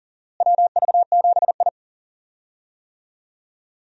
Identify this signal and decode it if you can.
Morse code, heard as a keyed tone, about 37 wpm, 700 Hz. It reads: WV7I
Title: USA Call Signs - 4 Characters - Repeated 3x - 40wpm